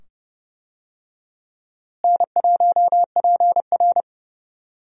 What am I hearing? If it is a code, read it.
Morse code, heard as a keyed tone, about 30 wpm, 700 Hz. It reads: N1PR
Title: USA Call Signs - 4 Characters - Repeated 2x - 30wpm